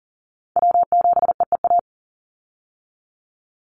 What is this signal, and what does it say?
Morse code, heard as a keyed tone, about 42 wpm, 700 Hz. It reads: W7EEA
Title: USA Call Signs 40wpm